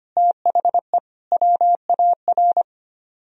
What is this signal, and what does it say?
Morse code, heard as a keyed tone, about 32 wpm, 700 Hz. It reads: THE WAR